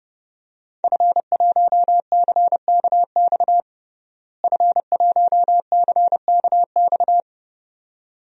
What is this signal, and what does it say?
Morse code, heard as a keyed tone, about 32 wpm, 700 Hz. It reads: F1CKX F1CKX
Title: DXCC-100 Call Signs - 5 Characters - Repeated 2x - 30wpm